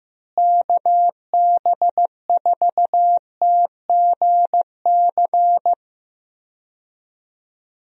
Morse code, heard as a keyed tone, about 15 wpm, 700 Hz. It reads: KB4TGC
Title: USA Call Signs - 6 Characters - Repeated 2x - 15wpm